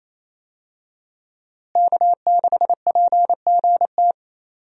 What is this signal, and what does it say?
Morse code, heard as a keyed tone, about 28 wpm, 700 Hz. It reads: K6PGT
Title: USA Call Signs - 5 Characters - Repeated 3x - 28wpm